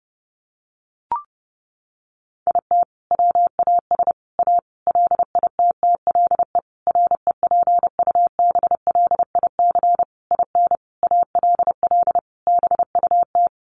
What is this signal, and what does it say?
Morse code, heard as a keyed tone, about 30 wpm, 700 Hz. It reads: IT WAS A LITTLE REPUBLIC IN ALL BUT
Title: Sentences from Top 1500 Words 30wpm